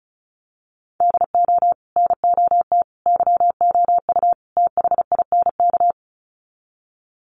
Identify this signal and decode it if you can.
Morse code, heard as a keyed tone, about 34 wpm, 700 Hz. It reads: DO NOT YOU THINK
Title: Sets of 4 Words 35wpm